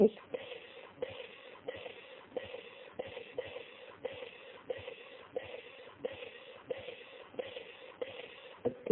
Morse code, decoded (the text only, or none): none